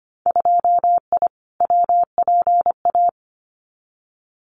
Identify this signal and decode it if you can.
Morse code, heard as a keyed tone, about 25 wpm, 700 Hz. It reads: 2I WPA
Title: Winter Field Day Exchanges 25wpm